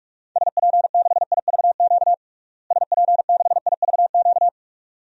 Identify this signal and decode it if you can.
Morse code, heard as a keyed tone, about 45 wpm, 700 Hz. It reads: SP6IVQ SP6IVQ
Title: DXCC-100 Call Signs - 6 Characters - Repeated 2x - 45wpm